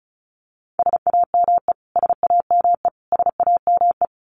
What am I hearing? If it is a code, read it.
Morse code, heard as a keyed tone, about 34 wpm, 700 Hz. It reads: SAME SAME SAME
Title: Top 500 Words Repeated 35wpm